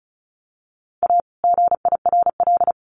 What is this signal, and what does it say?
Morse code, heard as a keyed tone, about 34 wpm, 700 Hz. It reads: A GIRL
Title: Sentences from Top 1400 Words 35wpm